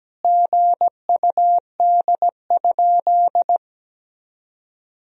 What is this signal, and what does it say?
Morse code, heard as a keyed tone, about 17 wpm, 700 Hz. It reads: GUD?
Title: Sentences from Top 200 Words and Abbreviations 17wpm